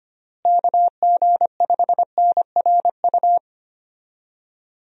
Morse code, heard as a keyed tone, about 25 wpm, 700 Hz. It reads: KG5NRU
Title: USA Call Signs - 6 Characters - Repeated 3x - 25wpm